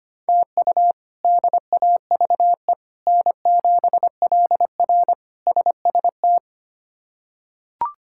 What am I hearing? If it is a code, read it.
Morse code, heard as a keyed tone, about 25 wpm, 700 Hz. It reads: TU DAVE N7LR SST
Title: K1USN SST QSO Practice - 25wpm